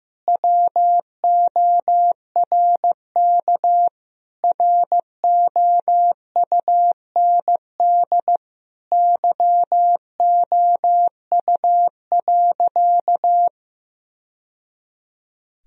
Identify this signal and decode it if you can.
Morse code, heard as a keyed tone, about 15 wpm, 700 Hz. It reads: WORK ROUND YOU.
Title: Sentences from Top 1000 Words - Review Part 1 - 15wpm